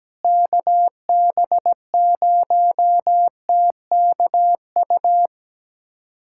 Morse code, heard as a keyed tone, about 17 wpm, 700 Hz. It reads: KB0TKU
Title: USA Callsign DE Callsign - 6 Characters - 17wpm